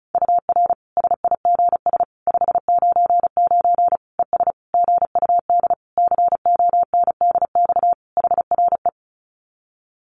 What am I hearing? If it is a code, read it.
Morse code, heard as a keyed tone, about 34 wpm, 700 Hz. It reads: UR SIGS 599 ES GUD CONDX HRE
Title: Common QSO Phrases 35wpm